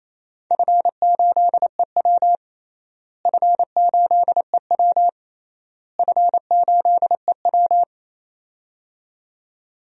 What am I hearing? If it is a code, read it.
Morse code, heard as a keyed tone, about 28 wpm, 700 Hz. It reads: F8EW F8EW F8EW
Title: DXCC-100 Call Signs - 4 Characters - Repeated 3x - 28wpm